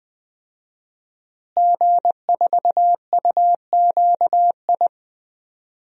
Morse code, heard as a keyed tone, about 20 wpm, 700 Hz. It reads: G4UQI
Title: DXCC-100 Call Signs - 5 Characters - Repeated 2x - 20wpm